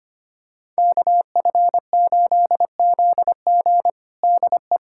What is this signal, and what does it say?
Morse code, heard as a keyed tone, about 25 wpm, 700 Hz. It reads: KF8ZG DE
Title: USA Callsign DE Callsign - 5 Characters - 25wpm